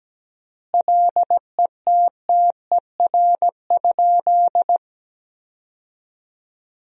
Morse code, heard as a keyed tone, about 17 wpm, 700 Hz. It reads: LETTER?
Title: 3x Word-Spacing - Sentences from Top 300 Words - 17wpm